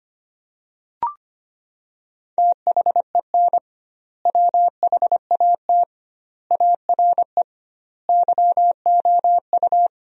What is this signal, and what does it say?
Morse code, heard as a keyed tone, about 25 wpm, 700 Hz. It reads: THEN WHAT ARE YOU
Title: 2x Word-Spacing - Sentences from Top 100 Words - 25wpm